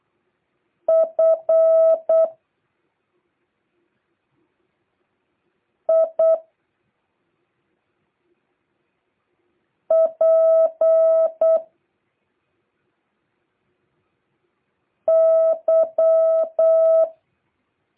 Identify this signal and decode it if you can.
Morse code, heard as a keyed tone, about 8 wpm, 650 Hz. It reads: F I P Y